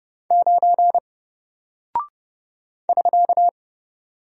Morse code, heard as a keyed tone, about 30 wpm, 700 Hz. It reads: 9 <SK>
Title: Single Character (Letter:Number:Punctuation:Prosign) Alt 30wpm